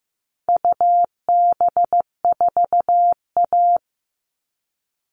Morse code, heard as a keyed tone, about 15 wpm, 700 Hz. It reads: UB4A
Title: All Cut Numbers - 4-Digit 15wpm